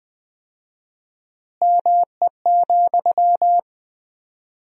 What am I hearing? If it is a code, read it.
Morse code, heard as a keyed tone, about 20 wpm, 700 Hz. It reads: ME,
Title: 4x Word-Spacing - Sentences from Top 200 Words - 20wpm